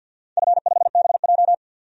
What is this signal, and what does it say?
Morse code, heard as a keyed tone, about 51 wpm, 700 Hz. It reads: F5BJ
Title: DXCC-100 Call Signs - 4 Characters - 50wpm